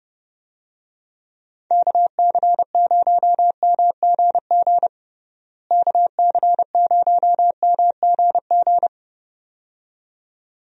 Morse code, heard as a keyed tone, about 30 wpm, 700 Hz. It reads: KC0MGG KC0MGG
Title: USA Call Signs - 6 Characters - Repeated 2x - 30wpm